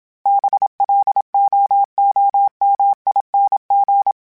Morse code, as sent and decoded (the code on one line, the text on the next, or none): -... .-.. --- --- -- .. -. --.
BLOOMING